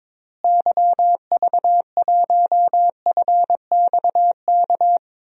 Morse code, heard as a keyed tone, about 21 wpm, 700 Hz. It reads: YV1FXK
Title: DXCC-100 Call Signs - 6 Characters - Repeated 3x - 22wpm